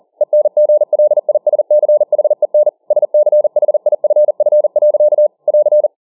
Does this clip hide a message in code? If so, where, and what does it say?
Yes: Morse code from 0.2 to 5.9 s, heard as a keyed tone, about 40 wpm, 600 Hz. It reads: ENGLISCHEN SCHIFF. <AR>